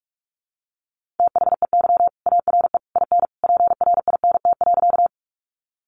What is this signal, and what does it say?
Morse code, heard as a keyed tone, about 44 wpm, 700 Hz. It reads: THEY ARE IN PRINT.
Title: Sentences from Top 1400 Words 45wpm